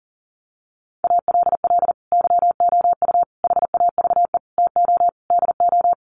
Morse code, heard as a keyed tone, about 40 wpm, 700 Hz. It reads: ALL YOU HAVE TO DO
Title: Sets of 5 Words 40wpm